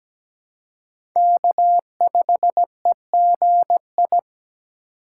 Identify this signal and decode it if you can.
Morse code, heard as a keyed tone, about 17 wpm, 700 Hz. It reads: K5EGI
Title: USA Call Signs - 4x Character Spacing - 17wpm